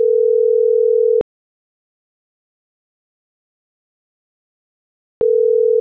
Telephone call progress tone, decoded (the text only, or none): ringback tone